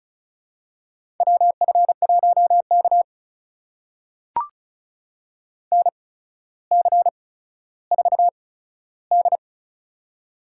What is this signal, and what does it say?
Morse code, heard as a keyed tone, about 36 wpm, spaced as for 15 wpm, 700 Hz. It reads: WF1K NC4D
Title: USA Call Signs - 4 Characters - 8x Character Spacing - 35wpm